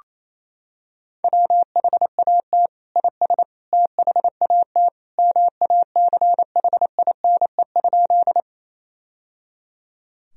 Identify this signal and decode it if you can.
Morse code, heard as a keyed tone, about 27 wpm, 700 Hz. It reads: WHAT IS THAT MACHINE?
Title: Sentences from Top 1000 Words - Review Part 2 - 28wpm